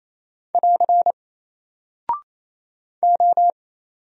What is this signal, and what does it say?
Morse code, heard as a keyed tone, about 28 wpm, 700 Hz. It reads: <AR> O
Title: Single Character (Letter:Number:Punctuation:Prosign) Alt 28wpm